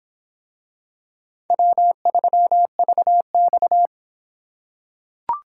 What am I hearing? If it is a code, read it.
Morse code, heard as a keyed tone, about 26 wpm, 700 Hz. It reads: W3VX